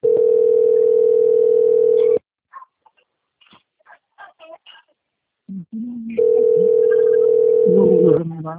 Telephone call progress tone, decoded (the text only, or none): ringback tone